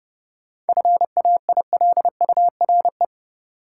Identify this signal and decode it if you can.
Morse code, heard as a keyed tone, about 30 wpm, 700 Hz. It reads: FAILURE
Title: Sentences from Top 900 Words - New Word Drill - 30wpm